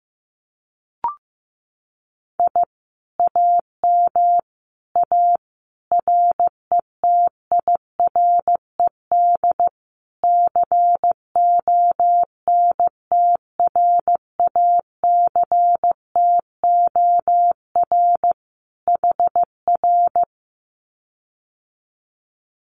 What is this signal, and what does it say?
Morse code, heard as a keyed tone, about 15 wpm, 700 Hz. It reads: I AM A RETIRED CONTRACTOR HR
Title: Common QSO Phrases Encore 15wpm